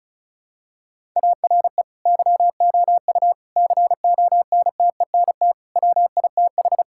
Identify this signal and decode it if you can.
Morse code, heard as a keyed tone, about 34 wpm, 700 Hz. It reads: ARE YOU CONTENT WITH
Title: Sentences from Top 1000 Words - Review Part 6 - 35wpm